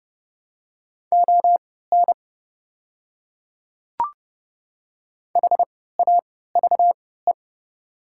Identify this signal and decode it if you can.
Morse code, heard as a keyed tone, about 30 wpm, spaced as for 17 wpm, 700 Hz. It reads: ON HAVE
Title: 3x Character Spacing - Top 300 Words - 30wpm